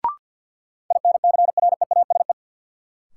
Morse code, heard as a keyed tone, about 53 wpm, 700 Hz. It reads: INCREASE
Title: Sets of 5 Words - New Word Drill - 50wpm